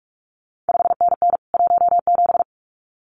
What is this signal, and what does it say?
Morse code, heard as a keyed tone, about 46 wpm, 700 Hz. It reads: 5NN 17
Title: CQ WW DX Contest Exchange Encore - 45wpm